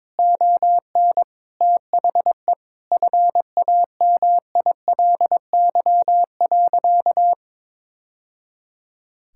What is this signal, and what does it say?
Morse code, heard as a keyed tone, about 22 wpm, 700 Hz. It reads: ON THE FAMILY.
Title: Sentences from Top 1000 Words - Review Part 6 - 22wpm